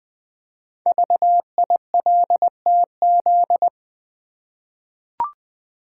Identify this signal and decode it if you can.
Morse code, heard as a keyed tone, about 20 wpm, 700 Hz. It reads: VILTZ
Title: Code Groups - 5 Letters - 20wpm